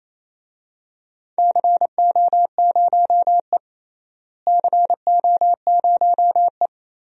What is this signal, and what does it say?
Morse code, heard as a keyed tone, about 28 wpm, 700 Hz. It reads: CO0E CO0E
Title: DXCC-100 Call Signs - 4 Characters - Repeated 3x - 28wpm